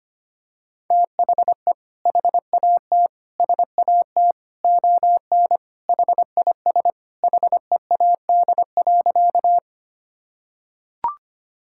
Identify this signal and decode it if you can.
Morse code, heard as a keyed tone, about 25 wpm, 700 Hz. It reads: THE HAT SAT ON HIS HEAD.
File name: practice / 25wpm / Sentences with 2-Word Phrases - 25wpm